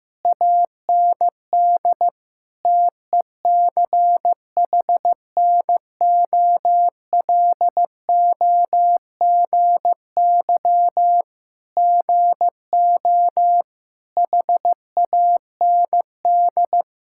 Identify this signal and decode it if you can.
Morse code, heard as a keyed tone, about 15 wpm, 700 Hz. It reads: AND TECHNOLOGY GO HAND
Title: Sentences from Top 1000 Words - Review Part 5 - 15wpm